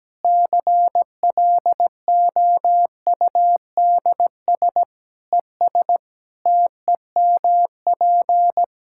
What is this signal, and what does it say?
Morse code, heard as a keyed tone, about 17 wpm, 700 Hz. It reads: CLOUDS ES TEMP
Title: Practice QSO - Exchange 2 - Info - 17wpm